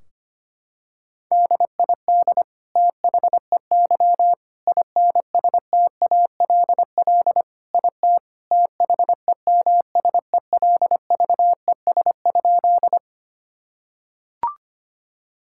Morse code, heard as a keyed tone, about 25 wpm, 700 Hz. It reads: DID THEY INSTALL IT THEMSELVES?
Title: Sentences from Top 1600 Words 25wpm